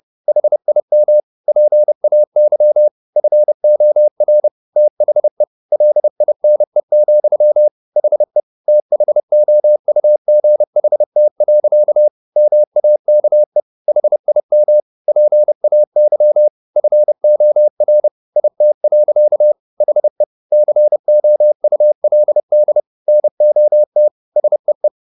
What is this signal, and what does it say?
Morse code, heard as a keyed tone, about 30 wpm, 600 Hz. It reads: HIM PAY FOR THE LINE, HE THOUGHT. MAKE HIM PAY FOR IT. HE COULD NOT SEE